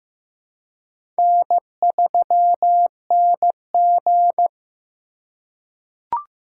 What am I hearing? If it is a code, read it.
Morse code, heard as a keyed tone, about 15 wpm, 700 Hz. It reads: N3NG
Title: USA Call Signs - 2x Character Spacing - 15wpm